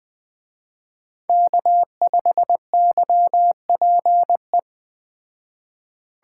Morse code, heard as a keyed tone, about 20 wpm, 700 Hz. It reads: K5YPE
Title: Speed Racing - USA 1x3 Call Signs 17wpm